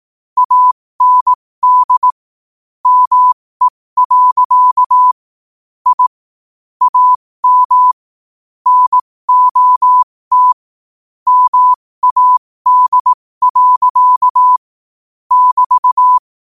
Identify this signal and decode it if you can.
Morse code, heard as a keyed tone, about 18 wpm, spaced as for 15 wpm, 1000 Hz. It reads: AND ME. I AM NOT MAD. <BT>
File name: qotdfiles / 15wpm